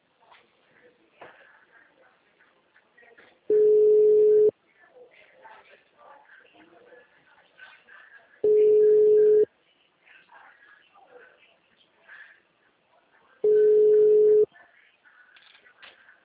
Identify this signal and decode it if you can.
Ringback tone (Europe, 425 Hz), with speech.